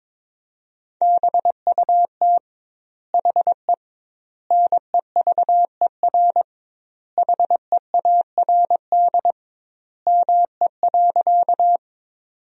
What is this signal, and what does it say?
Morse code, heard as a keyed tone, about 22 wpm, 700 Hz. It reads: BUT HE NEVER HEARD ME.
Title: 2x Word-Spacing - Sentences from Top 400 Words - 22wpm